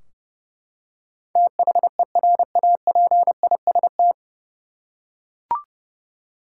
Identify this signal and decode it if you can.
Morse code, heard as a keyed tone, about 31 wpm, 700 Hz. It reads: THERAPIST
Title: Practice QSO - Exchange 3 - More Info - 30wpm